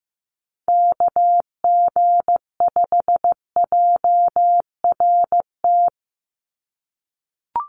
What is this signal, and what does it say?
Morse code, heard as a keyed tone, about 15 wpm, 700 Hz. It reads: KG5JRT
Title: USA Call Signs 15wpm